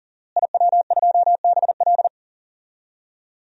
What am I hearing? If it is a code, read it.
Morse code, heard as a keyed tone, about 38 wpm, 700 Hz. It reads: IW2BL
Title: DXCC-100 Call Signs - 5 Characters - Repeated 3x - 40wpm